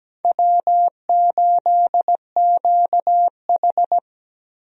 Morse code, heard as a keyed tone, about 17 wpm, 700 Hz. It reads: W8QH
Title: USA Call Signs - 4 Characters - 17wpm